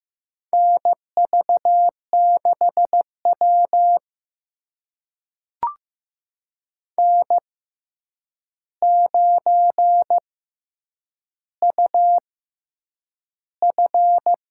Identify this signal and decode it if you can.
Morse code, heard as a keyed tone, about 15 wpm, spaced as for 8 wpm, 700 Hz. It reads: NV6W N9UF
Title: USA Call Signs - 4 Characters - 6x Character Spacing - 15wpm